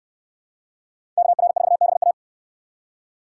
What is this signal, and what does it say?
Morse code, heard as a keyed tone, about 59 wpm, 700 Hz. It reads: KR4BA